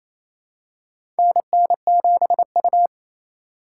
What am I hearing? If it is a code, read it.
Morse code, heard as a keyed tone, about 28 wpm, 700 Hz. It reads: NN7U
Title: USA Call Signs - 4 Characters - 3x Character Spacing - 28wpm